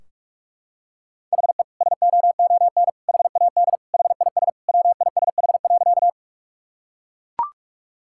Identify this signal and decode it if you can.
Morse code, heard as a keyed tone, about 44 wpm, 700 Hz. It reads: HE SOON HAD HIS WISH.
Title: Sentences from Top 1600 Words 45wpm